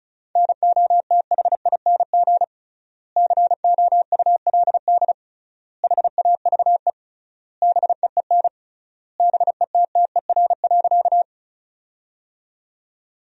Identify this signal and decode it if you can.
Morse code, heard as a keyed tone, about 36 wpm, 700 Hz. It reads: NOTHING COULD HAVE BEEN BETTER.
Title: 3x Word-Spacing - Sentences from Top 500 Words - 35wpm